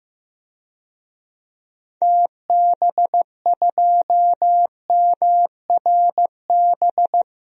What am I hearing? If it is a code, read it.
Morse code, heard as a keyed tone, about 15 wpm, 700 Hz. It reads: TB2MRB
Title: DXCC-100 Call Signs - 6 Characters - Repeated 2x - 15wpm